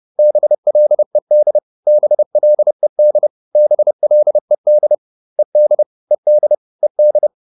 Morse code, heard as a keyed tone, about 30 wpm, 600 Hz. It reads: BLED BLED BLED ED ED ED
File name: ED4 30